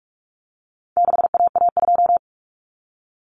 Morse code, heard as a keyed tone, about 44 wpm, 700 Hz. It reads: 6AA2
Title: Common Cut Numbers - 4-Digit 45wpm